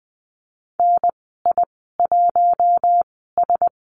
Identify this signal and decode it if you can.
Morse code, heard as a keyed tone, about 20 wpm, spaced as for 15 wpm, 700 Hz. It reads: NI1S
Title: 042 Lesson 8g - Number 1 - Call Signs 20wpm